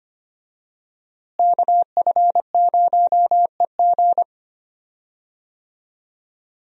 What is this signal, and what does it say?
Morse code, heard as a keyed tone, about 25 wpm, 700 Hz. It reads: KF0EG
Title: USA Call Signs - 5 Characters - Repeated 2x - 25wpm